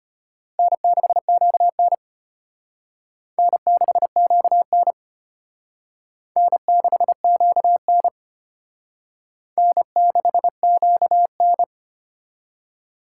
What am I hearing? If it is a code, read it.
Morse code, heard as a keyed tone, about 31 wpm, 700 Hz. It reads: N6QN N6QN N6QN N6QN
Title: Speed Racing - USA 1x2 Call Signs 25wpm